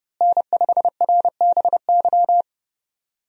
Morse code, heard as a keyed tone, about 30 wpm, 700 Hz. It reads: N5RBY
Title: USA Call Signs - 5 Characters - Repeated 3x - 30wpm